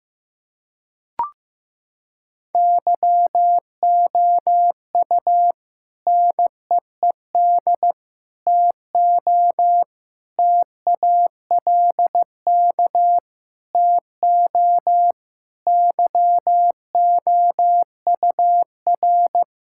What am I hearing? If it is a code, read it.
Morse code, heard as a keyed tone, about 15 wpm, 700 Hz. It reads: YOU NEED TO TALK TO YOUR
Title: Sentences from Top 1600 Words 15wpm